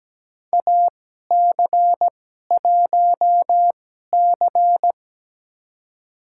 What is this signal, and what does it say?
Morse code, heard as a keyed tone, about 17 wpm, spaced as for 12 wpm, 700 Hz. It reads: AC1C
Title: USA Call Signs - 2x Character Spacing - 17wpm